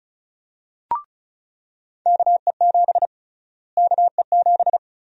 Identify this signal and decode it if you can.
Morse code, heard as a keyed tone, about 36 wpm, 700 Hz. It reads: KE7 KE7
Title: USA Call Sign Prefixes - Repeated 2x - 35wpm